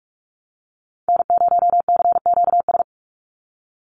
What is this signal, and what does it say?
Morse code, heard as a keyed tone, about 44 wpm, 700 Hz. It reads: N0CQS
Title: USA Call Signs 45wpm